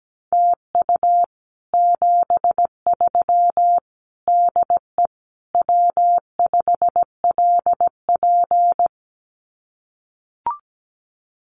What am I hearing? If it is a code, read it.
Morse code, heard as a keyed tone, about 17 wpm, 700 Hz. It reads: TU 73 DE W5LP
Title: Common QSO Phrases Encore 17wpm